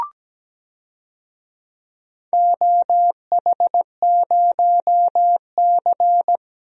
Morse code, heard as a keyed tone, about 17 wpm, 700 Hz. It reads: OH0C
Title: DXCC-100 Call Signs - 4 Characters - Repeated 2x - 17wpm